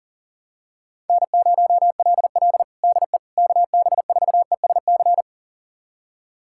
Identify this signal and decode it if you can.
Morse code, heard as a keyed tone, about 40 wpm, 700 Hz. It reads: N0LL DE KB4ESC
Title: USA Callsign DE Callsign 40wpm